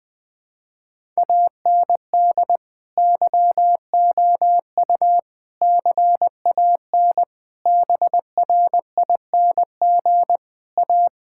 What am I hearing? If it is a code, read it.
Morse code, heard as a keyed tone, about 21 wpm, 700 Hz. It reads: AND YOU CAN BRING A
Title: Sentences with 2-Word Phrases - 20wpm